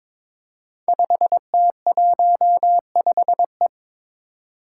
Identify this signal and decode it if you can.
Morse code, heard as a keyed tone, about 22 wpm, 700 Hz. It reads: 5T15E
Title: Five Challenging Letter-Number Combinations 22wpm